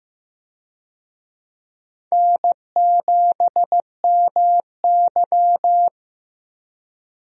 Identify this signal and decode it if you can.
Morse code, heard as a keyed tone, about 15 wpm, 700 Hz. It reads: N7MY
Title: USA Call Signs - 4 Characters - 15wpm